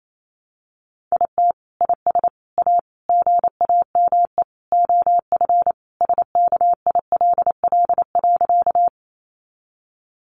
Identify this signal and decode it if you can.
Morse code, heard as a keyed tone, about 28 wpm, 700 Hz. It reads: IT IS A GAME OF SKILL.
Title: Sentences from Top 1200 Words 28wpm